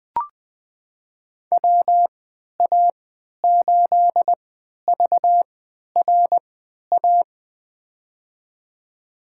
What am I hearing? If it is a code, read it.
Morse code, heard as a keyed tone, about 20 wpm, spaced as for 11 wpm, 700 Hz. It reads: WA8VRA
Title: USA Call Signs - 6 Characters - 3x Character Spacing - 20wpm